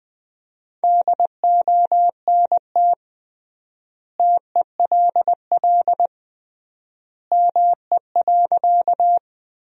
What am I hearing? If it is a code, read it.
Morse code, heard as a keyed tone, about 19 wpm, 700 Hz. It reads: DONT TELL ME.
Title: 3x Word-Spacing - Sentences from Top 300 Words - 20wpm